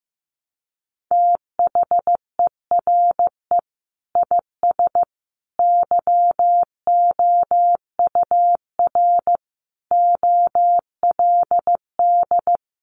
Morse code, heard as a keyed tone, about 14 wpm, 700 Hz. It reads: THERE IS YOUR OLD
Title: Sentences from Top 500 Words 15wpm